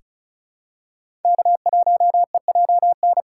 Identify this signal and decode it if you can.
Morse code, heard as a keyed tone, about 34 wpm, 700 Hz. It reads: K1EJN
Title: USA Call Signs - 5 Characters - 3x Character Spacing - 35wpm